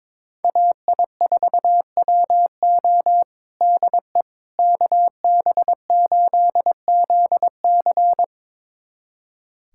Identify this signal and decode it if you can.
Morse code, heard as a keyed tone, about 22 wpm, 700 Hz. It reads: AI4WO DE KB8ZC
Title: USA Callsign DE Callsign - 5 Characters - 22wpm